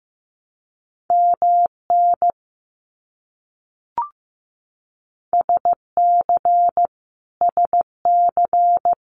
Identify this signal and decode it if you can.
Morse code, heard as a keyed tone, about 15 wpm, 700 Hz. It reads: MN SC SC
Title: US States Repeated 3x 15wpm